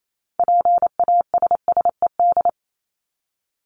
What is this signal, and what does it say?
Morse code, heard as a keyed tone, about 28 wpm, 700 Hz. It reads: PASSED
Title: ED Suffix - Top 100 Words with Base Words 28wpm